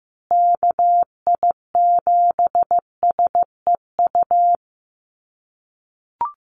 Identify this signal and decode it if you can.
Morse code, heard as a keyed tone, about 15 wpm, 700 Hz. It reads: KI7SEU
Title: USA Call Signs Repeated 3x 15wpm